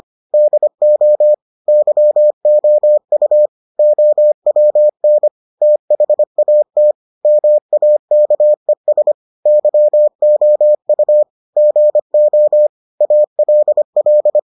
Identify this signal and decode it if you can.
Morse code, heard as a keyed tone, about 25 wpm, 600 Hz. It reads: DO YOU OWN THAT MAKES YOU GO ALL